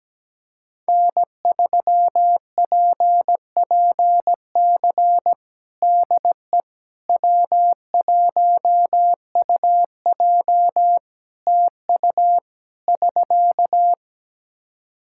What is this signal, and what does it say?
Morse code, heard as a keyed tone, about 17 wpm, 700 Hz. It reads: N3PPC DE W1UJ TU <SK>
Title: Practice QSO - Exchange 4 - Ending - 17wpm